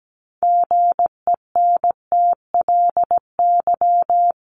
Morse code, heard as a keyed tone, about 17 wpm, 700 Hz. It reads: GENTLY
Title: Sentences from Top 1500 Words 17wpm